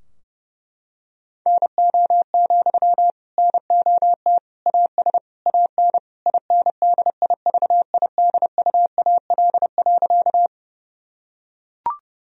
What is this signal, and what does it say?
Morse code, heard as a keyed tone, about 30 wpm, 700 Hz. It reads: NO, NOT AS AN INDIVIDUAL.
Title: Sentences from Top 1000 Words - Review Part 4 - 30wpm